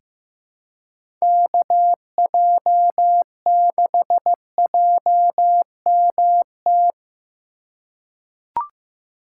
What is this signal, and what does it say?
Morse code, heard as a keyed tone, about 15 wpm, 700 Hz. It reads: KJ6JMT
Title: USA Call Signs - 6 Characters - 4x Character Spacing - 15wpm